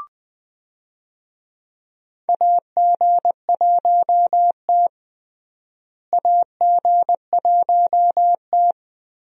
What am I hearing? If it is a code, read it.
Morse code, heard as a keyed tone, about 20 wpm, 700 Hz. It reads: AG1T AG1T
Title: USA Call Signs - 4 Characters - Repeated 3x - 20wpm